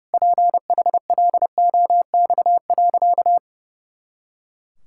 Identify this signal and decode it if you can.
Morse code, heard as a keyed tone, about 30 wpm, 700 Hz. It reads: PHLOX.
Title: Pangram Challenge v2 - 30wpm